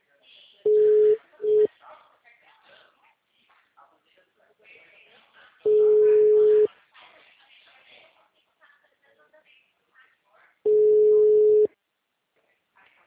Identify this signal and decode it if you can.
Ringback tone (Europe, 425 Hz), with speech.